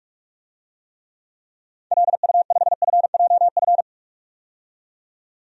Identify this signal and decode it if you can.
Morse code, heard as a keyed tone, about 46 wpm, 700 Hz. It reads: LU5FJF
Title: DXCC-100 Call Signs - 6 Characters - 45wpm